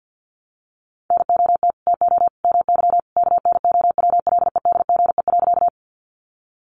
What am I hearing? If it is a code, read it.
Morse code, heard as a keyed tone, about 50 wpm, 700 Hz. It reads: NOT TO MY KNOWLEDGE.
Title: Sentences from Top 600 Words 50wpm